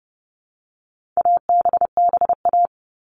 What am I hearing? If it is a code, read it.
Morse code, heard as a keyed tone, about 30 wpm, 700 Hz. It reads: ABBA
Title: All Cut Numbers - 4-Digit 30wpm